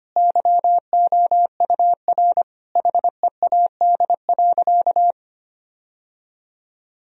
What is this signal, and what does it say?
Morse code, heard as a keyed tone, about 25 wpm, 700 Hz. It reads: YOUR HEAD.
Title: Sentences from Top 1000 Words - Review Part 1 - 25wpm